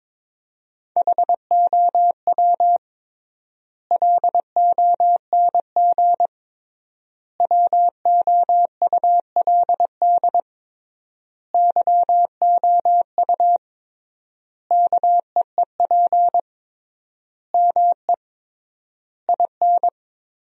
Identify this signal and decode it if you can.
Morse code, heard as a keyed tone, about 22 wpm, 700 Hz. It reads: HOW LONG WOULD YOU KEEP ME IN
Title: 3x Word-Spacing - Sentences from Top 500 Words - 22wpm